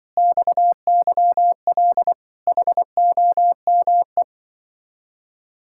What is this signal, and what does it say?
Morse code, heard as a keyed tone, about 25 wpm, 700 Hz. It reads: XYL HOME